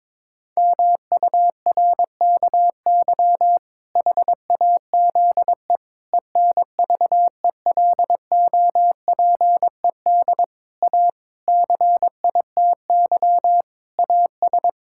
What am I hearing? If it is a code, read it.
Morse code, heard as a keyed tone, about 22 wpm, 700 Hz. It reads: MURKY HAZE ENVELOPED A CITY AS
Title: Pangram Challenge v2 - 22wpm